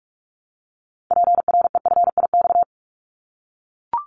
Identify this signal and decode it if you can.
Morse code, heard as a keyed tone, about 44 wpm, 700 Hz. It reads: PREFIX
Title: Callsign Prefixes for Top 100 DXCC - 45wpm